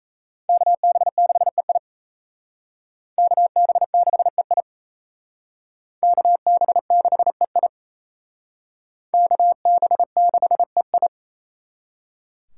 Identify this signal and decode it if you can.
Morse code, heard as a keyed tone, about 35 wpm, 700 Hz. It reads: KB6EI KB6EI KB6EI KB6EI
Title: Speed Racing - USA 2x2 Call Signs 28wpm